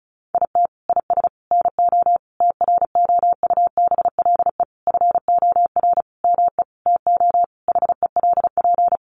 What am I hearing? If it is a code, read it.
Morse code, heard as a keyed tone, about 34 wpm, 700 Hz. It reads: IT IS NO TROUBLE FOR ME TO HELP
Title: Sentences from Top 1200 Words 35wpm